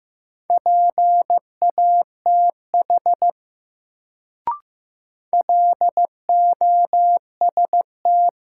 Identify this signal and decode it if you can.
Morse code, heard as a keyed tone, about 15 wpm, 700 Hz. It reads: PATH LOST
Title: Sentences from Top 1400 Words - New Word Drill Alt - 15wpm